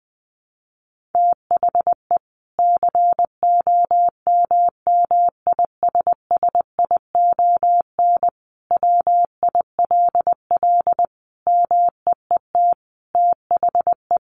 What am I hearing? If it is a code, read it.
Morse code, heard as a keyed tone, about 20 wpm, 700 Hz. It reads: THE COMMISSION WILL MEET THE
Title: Sentences from Top 1400 Words 20wpm